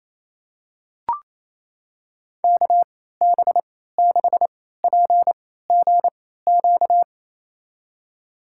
Morse code, heard as a keyed tone, about 28 wpm, spaced as for 16 wpm, 700 Hz. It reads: KB6PGQ
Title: USA Call Signs - 3x Character Spacing - 28wpm